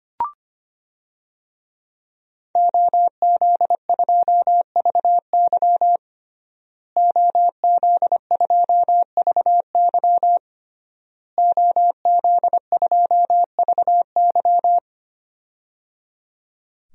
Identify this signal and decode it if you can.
Morse code, heard as a keyed tone, about 25 wpm, 700 Hz. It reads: OZ2VY OZ2VY OZ2VY
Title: DXCC-100 Call Signs - 5 Characters - Repeated 3x - 25wpm